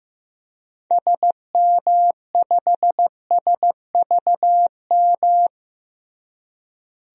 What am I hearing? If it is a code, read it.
Morse code, heard as a keyed tone, about 15 wpm, 700 Hz. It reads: SM5SVM
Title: DXCC-100 Call Signs - 6 Characters - Repeated 2x - 15wpm